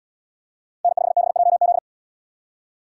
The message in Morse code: N5BJZ

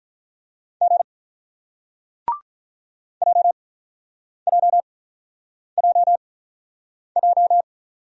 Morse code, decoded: G JJJJ

